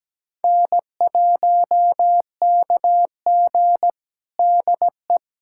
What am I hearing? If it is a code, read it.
Morse code, heard as a keyed tone, about 17 wpm, 700 Hz. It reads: N1KG DE
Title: USA Callsign DE Callsign - 4 Characters - 17wpm